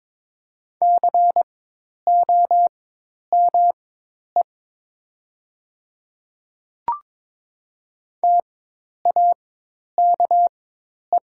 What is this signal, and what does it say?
Morse code, heard as a keyed tone, about 21 wpm, spaced as for 10 wpm, 700 Hz. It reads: COME TAKE
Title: 4x Character Spacing - Top 300 Words - 22wpm